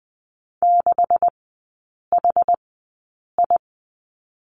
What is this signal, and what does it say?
Morse code, heard as a keyed tone, about 20 wpm, spaced as for 8 wpm, 700 Hz. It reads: 6HI
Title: 152 Lesson 25e - Letter p - Three Characters 20wpm